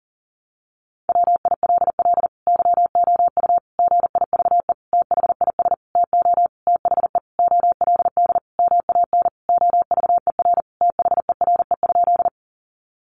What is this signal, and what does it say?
Morse code, heard as a keyed tone, about 40 wpm, 700 Hz. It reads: WILL YOU GIVE THIS TO THE OLD MAN OVER THERE?
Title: Sentences from Top 200 Words 40wpm